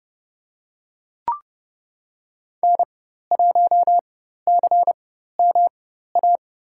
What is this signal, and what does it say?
Morse code, heard as a keyed tone, about 30 wpm, spaced as for 14 wpm, 700 Hz. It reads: N1CMA
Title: USA Call Signs - 4x Character Spacing - 30wpm